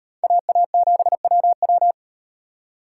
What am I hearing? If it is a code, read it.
Morse code, heard as a keyed tone, about 38 wpm, 700 Hz. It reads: AA7WW